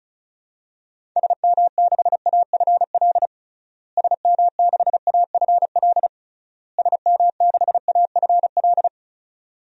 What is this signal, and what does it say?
Morse code, heard as a keyed tone, about 34 wpm, 700 Hz. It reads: SM6AFL SM6AFL SM6AFL
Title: DXCC-100 Call Signs - 6 Characters - Repeated 3x - 35wpm